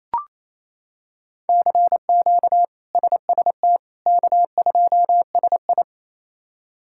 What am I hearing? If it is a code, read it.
Morse code, heard as a keyed tone, about 28 wpm, 700 Hz. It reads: CQ SST K2SI
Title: K1USN SST QSO Practice - 28wpm